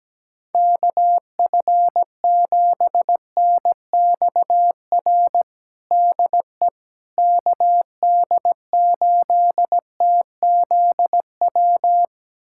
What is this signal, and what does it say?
Morse code, heard as a keyed tone, about 17 wpm, 700 Hz. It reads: KF7NXR DE KD8TZW
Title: USA Callsign DE Callsign - 6 Characters - 17wpm